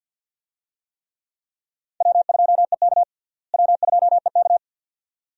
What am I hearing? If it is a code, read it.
Morse code, heard as a keyed tone, about 47 wpm, 700 Hz. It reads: W2EK W2EK
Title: USA Call Signs - 4 Characters - Repeated 2x - 50wpm